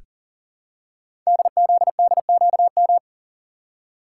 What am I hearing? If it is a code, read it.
Morse code, heard as a keyed tone, about 39 wpm, 700 Hz. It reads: DZDQM
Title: Code Groups - 5 Letters - 40wpm